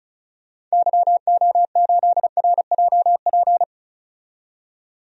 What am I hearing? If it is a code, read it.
Morse code, heard as a keyed tone, about 36 wpm, 700 Hz. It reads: YO8RJP
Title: DXCC-100 Call Signs - 6 Characters - 35wpm